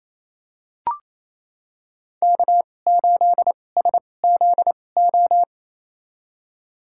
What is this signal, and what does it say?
Morse code, heard as a keyed tone, about 28 wpm, spaced as for 20 wpm, 700 Hz. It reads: K8SZO
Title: USA Call Signs - 5 Characters - 2x Character Spacing - 28wpm